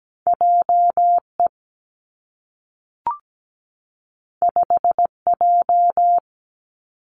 Morse code, heard as a keyed tone, about 17 wpm, 700 Hz. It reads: JE 5J
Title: Two Challenging Letter-Number Combinations 17wpm